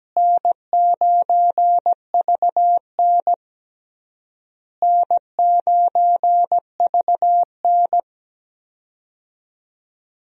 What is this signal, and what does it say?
Morse code, heard as a keyed tone, about 17 wpm, 700 Hz. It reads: N9VN N9VN
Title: USA Call Signs - 4 Characters - Repeated 3x - 17wpm